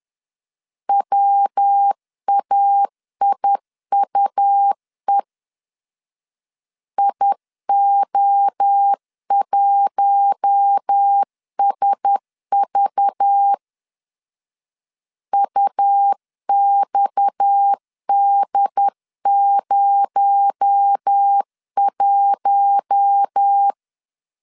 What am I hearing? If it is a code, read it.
Morse code, heard as a keyed tone, about 10 wpm, 800 Hz. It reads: WAIUE IO1SV UXD01
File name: CW / DET / 1 6 1500C S C 10wpm 4252